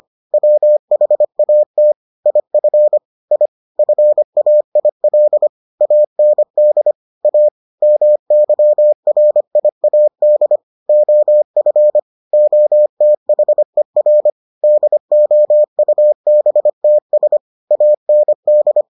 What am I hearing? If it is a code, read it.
Morse code, heard as a keyed tone, about 25 wpm, 600 Hz. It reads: WHAT IF I FAIL AND A MYRIAD OF OTHER DOUBTS AND